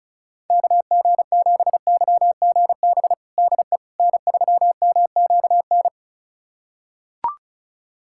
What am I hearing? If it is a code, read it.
Morse code, heard as a keyed tone, about 34 wpm, 700 Hz. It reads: KG7YGB DE N3MQN
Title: USA Callsign DE Callsign 35wpm